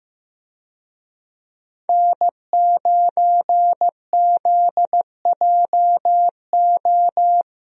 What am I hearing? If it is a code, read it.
Morse code, heard as a keyed tone, about 14 wpm, 700 Hz. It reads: N9ZJO
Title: USA Call Signs - 5 Characters - Repeated 3x - 15wpm